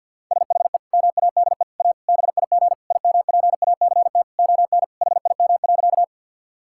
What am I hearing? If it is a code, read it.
Morse code, heard as a keyed tone, about 50 wpm, 700 Hz. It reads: SHE MADE A BIG IMPACT ON HIM.
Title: Sentences from Top 1000 Words - Review Part 6 - 50wpm